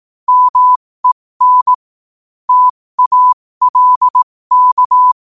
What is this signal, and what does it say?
Morse code, heard as a keyed tone, about 18 wpm, spaced as for 15 wpm, 1000 Hz. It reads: MEN TALK
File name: qotdfiles / 15wpm